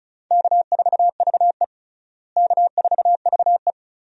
Morse code, heard as a keyed tone, about 36 wpm, 700 Hz. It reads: K4VE K4VE
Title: USA Call Signs - 4 Characters - Repeated 2x - 35wpm